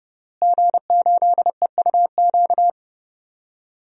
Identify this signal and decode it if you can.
Morse code, heard as a keyed tone, about 30 wpm, 700 Hz. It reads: G8EUQ
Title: DXCC-100 Call Signs - 5 Characters - Repeated 3x - 30wpm